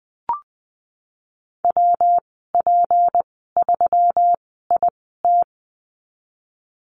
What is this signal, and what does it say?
Morse code, heard as a keyed tone, about 19 wpm, spaced as for 14 wpm, 700 Hz. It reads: WP3IT
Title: 167 Lesson 27g - Number 7 - Call Signs 20wpm